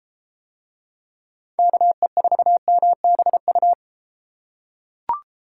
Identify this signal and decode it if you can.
Morse code, heard as a keyed tone, about 32 wpm, 700 Hz. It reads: KE4MBU